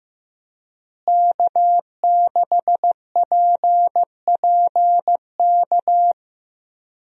Morse code, heard as a keyed tone, about 15 wpm, 700 Hz. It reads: K6PPK
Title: USA Call Signs - 5 Characters - Repeated 3x - 15wpm